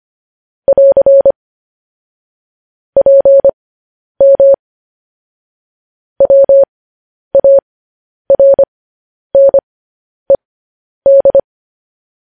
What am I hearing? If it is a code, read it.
Morse code, heard as a keyed tone, about 25 wpm, spaced as for 10 wpm, 550 Hz. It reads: <AR> PM WARNED